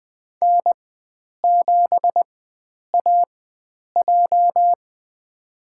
Morse code, heard as a keyed tone, about 20 wpm, spaced as for 9 wpm, 700 Hz. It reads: N7AJ
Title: USA Call Signs - 4 Characters - 4x Character Spacing - 20wpm